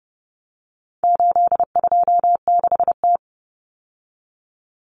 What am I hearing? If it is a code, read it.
Morse code, heard as a keyed tone, about 30 wpm, 700 Hz. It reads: 826T
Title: Common Cut Numbers - 4-Digit 30wpm